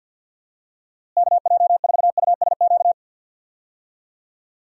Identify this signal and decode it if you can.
Morse code, heard as a keyed tone, about 51 wpm, 700 Hz. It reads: KJ4USQ
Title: USA Call Signs - 6 Characters - Repeated 2x - 50wpm